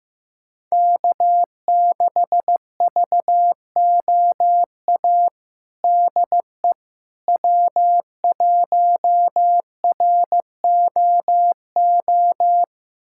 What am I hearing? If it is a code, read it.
Morse code, heard as a keyed tone, about 15 wpm, 700 Hz. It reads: K6VOA DE W1ROO
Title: USA Callsign DE Callsign - 5 Characters - 15wpm